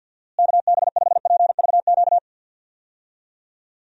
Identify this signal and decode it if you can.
Morse code, heard as a keyed tone, about 52 wpm, 700 Hz. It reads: KB5PVQ
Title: USA Call Signs - 6 Characters - Repeated 3x - 50wpm